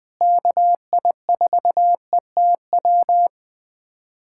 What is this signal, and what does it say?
Morse code, heard as a keyed tone, about 19 wpm, 700 Hz. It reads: KI4ETW
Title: USA Callsign DE Callsign - 6 Characters - 20wpm